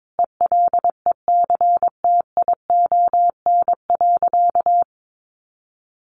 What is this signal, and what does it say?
Morse code, heard as a keyed tone, about 22 wpm, 700 Hz. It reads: ELECTION.
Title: Sentences from Top 1500 Words 22wpm